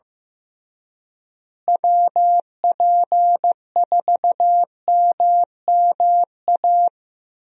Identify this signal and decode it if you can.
Morse code, heard as a keyed tone, about 15 wpm, 700 Hz. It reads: WP4MMA
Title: USA Call Signs - 6 Characters - Repeated 2x - 15wpm